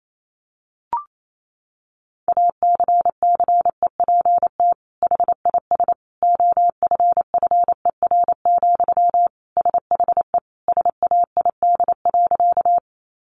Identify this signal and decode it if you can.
Morse code, heard as a keyed tone, about 28 wpm, 700 Hz. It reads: ACCEPT HIS OFFER, SHE SAID.
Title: Sentences from Top 1300 Words 28wpm